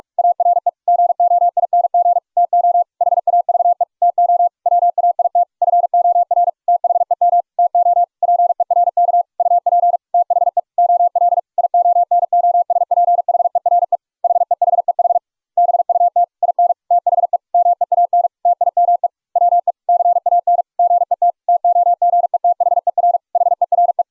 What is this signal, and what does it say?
Morse code, heard as a keyed tone, about 41 wpm, 700 Hz. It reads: ARE GOING TO HAVE TO WAIT FOR THEM TO PERK UP THE OL IONOSPHERE HEHEH BUT IN THE MEAN TIME WE CAN GET TOGETHER HERE